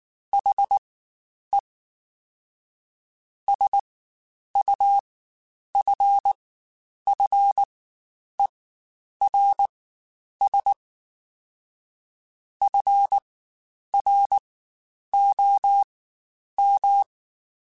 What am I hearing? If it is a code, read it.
Morse code, heard as a keyed tone, about 19 wpm, spaced as for 9 wpm, 800 Hz. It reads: HE SUFFERS FROM
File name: Quotes Spaced / Quotes Spaced-mp3s / HoofInMouth sp @18